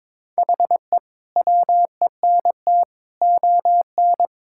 Morse code, heard as a keyed tone, about 22 wpm, 700 Hz. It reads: HE WENT ON
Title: Sentences from Top 1000 Words - Review Part 6 - 22wpm